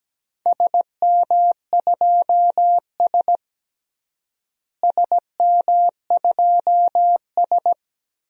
Morse code, heard as a keyed tone, about 17 wpm, 700 Hz. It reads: SM2S SM2S
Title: DXCC-100 Call Signs - 4 Characters - Repeated 2x - 17wpm